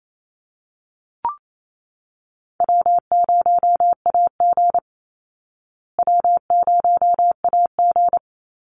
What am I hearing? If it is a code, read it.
Morse code, heard as a keyed tone, about 28 wpm, 700 Hz. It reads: W0AG W0AG
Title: Easy Call Signs Repeated 3x - 28wpm